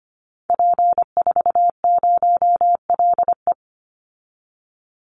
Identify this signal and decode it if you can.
Morse code, heard as a keyed tone, about 25 wpm, 700 Hz. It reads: P40LE
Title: Call Signs Repeated 3x - 25wpm